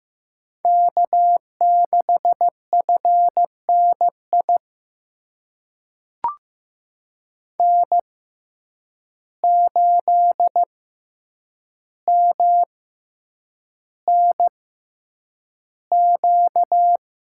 Morse code, heard as a keyed tone, about 15 wpm, spaced as for 8 wpm, 700 Hz. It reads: K6FNI N8MNQ
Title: USA Call Signs - 5 Characters - 6x Character Spacing - 15wpm